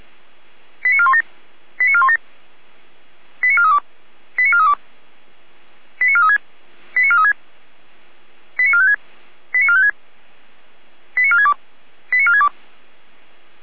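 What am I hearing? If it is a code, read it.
Five-tone selective call (ZVEI), heard as quick runs of five tones, tones of about 70 ms. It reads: 89417 89417 89432 89432 89436 89436 89456 89456 89462 89462
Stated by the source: ZVEI five-tone selective call received by SDR.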